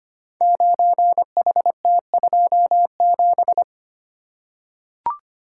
Morse code, heard as a keyed tone, about 25 wpm, 700 Hz. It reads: 9HT27
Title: Code Groups - 5 Letter:Numbers - 25wpm